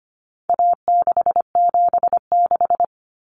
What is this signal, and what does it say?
Morse code, heard as a keyed tone, about 25 wpm, 700 Hz. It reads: A676
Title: Common Cut Numbers - 4-Digit 25wpm